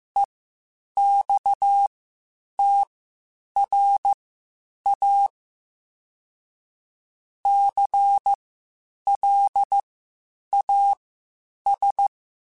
Morse code, heard as a keyed tone, about 15 wpm, spaced as for 8 wpm, 800 Hz. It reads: EXTRA CLAS